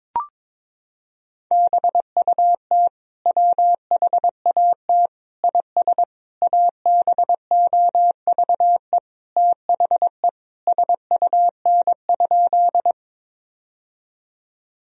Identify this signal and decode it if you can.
Morse code, heard as a keyed tone, about 22 wpm, 700 Hz. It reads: BUT WHAT IS ABOVE THE SUN?
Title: Sentences from Top 1000 Words - Review Part 2 - 22wpm